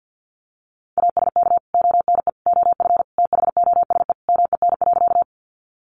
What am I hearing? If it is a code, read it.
Morse code, heard as a keyed tone, about 50 wpm, 700 Hz. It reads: ASK ONE OF THOSE MEN.
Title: Sentences from Top 300 Words 50wpm